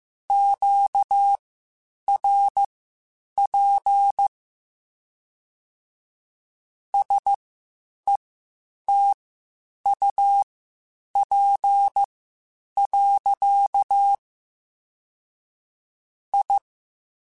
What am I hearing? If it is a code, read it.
Morse code, heard as a keyed tone, about 15 wpm, spaced as for 8 wpm, 800 Hz. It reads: QRP SETUP. I
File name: QSOs / QSOs-mp3s / WD0BGP 1@10